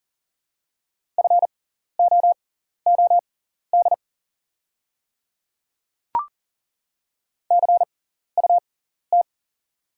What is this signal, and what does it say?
Morse code, heard as a keyed tone, about 40 wpm, spaced as for 14 wpm, 690 Hz. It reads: FOOD CUT
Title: 6x Character Spacing - Top 500 Words - 40wpm